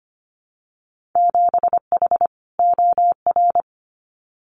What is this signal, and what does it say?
Morse code, heard as a keyed tone, about 25 wpm, 700 Hz. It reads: 7H OR
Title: Winter Field Day Exchanges Encore 25wpm